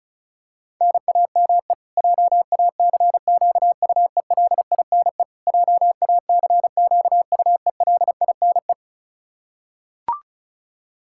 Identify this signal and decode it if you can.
Morse code, heard as a keyed tone, about 34 wpm, 700 Hz. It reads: NAME JACQUELINE JACQUELINE
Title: Practice QSO - Exchange 1 - Beginning - 35wpm